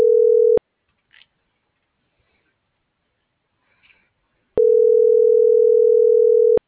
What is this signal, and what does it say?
Ringback tone (North America), 440 + 480 Hz.
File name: CENTRAL SAGOT / out xquenda.sagot - .85589